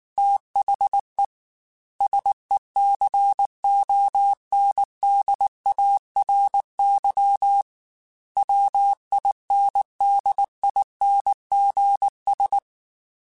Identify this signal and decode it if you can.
Morse code, heard as a keyed tone, about 19 wpm, 800 Hz. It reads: THE SECONDARY WINDINGS
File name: Technical Subjects / Technical Subjects-mp3s / Sw Pwr Sup @18